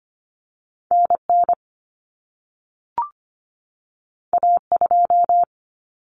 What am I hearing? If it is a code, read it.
Morse code, heard as a keyed tone, about 25 wpm, 700 Hz. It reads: NN A2
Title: Common Cut Numbers - 2-Digit 25wpm